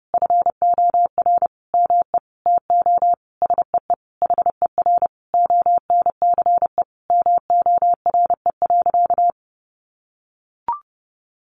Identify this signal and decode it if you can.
Morse code, heard as a keyed tone, about 30 wpm, 700 Hz. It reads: FOR ME TO SEE HER ONCE MORE.
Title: Sentences from Top 500 Words 30wpm